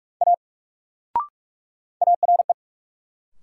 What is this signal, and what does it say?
Morse code, heard as a keyed tone, about 42 wpm, 700 Hz. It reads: A ARE
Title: Sentences from Top 100 Words - New Word Drill - 45wpm